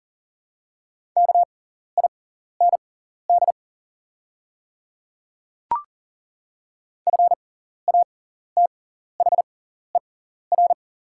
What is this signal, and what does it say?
Morse code, heard as a keyed tone, about 40 wpm, spaced as for 14 wpm, 690 Hz. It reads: KIND FATHER
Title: 6x Character Spacing - Top 500 Words - 40wpm